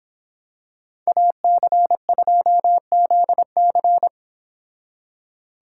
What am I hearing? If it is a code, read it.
Morse code, heard as a keyed tone, about 26 wpm, 700 Hz. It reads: AC2ZC